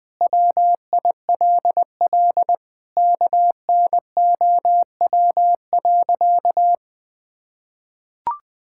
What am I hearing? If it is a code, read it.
Morse code, heard as a keyed tone, about 20 wpm, 700 Hz. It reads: WILL KNOW.
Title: Sentences with 4-Word Phrases - 20wpm